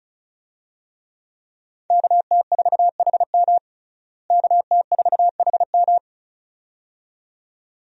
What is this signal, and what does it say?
Morse code, heard as a keyed tone, about 36 wpm, 700 Hz. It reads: KT4HM KT4HM
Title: USA Call Signs - 5 Characters - Repeated 2x - 35wpm